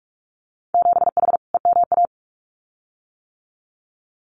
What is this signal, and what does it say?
Morse code, heard as a keyed tone, about 46 wpm, 700 Hz. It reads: 7H EMA
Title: Winter Field Day Exchanges Encore 45wpm